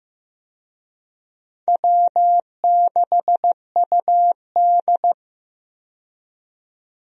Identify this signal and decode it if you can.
Morse code, heard as a keyed tone, about 15 wpm, 700 Hz. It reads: W6UD
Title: USA Call Signs - 4 Characters - 15wpm